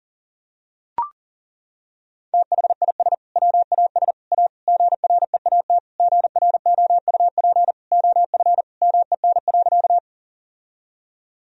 Morse code, heard as a keyed tone, about 40 wpm, 700 Hz. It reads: THIS WAS A GREAT GROUP OF MEN.
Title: Sentences from Top 1000 Words - Review Part 1 - 40wpm